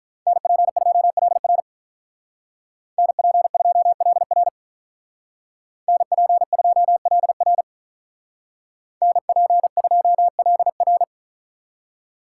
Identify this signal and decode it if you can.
Morse code, heard as a keyed tone, about 43 wpm, 700 Hz. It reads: NP2LR NP2LR NP2LR NP2LR
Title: Speed Racing - USA 2x2 Call Signs 35wpm